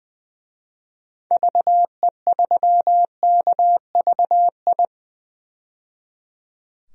Morse code, heard as a keyed tone, about 20 wpm, 700 Hz. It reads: VE3KVI
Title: DXCC-100 Call Signs - 6 Characters - 20wpm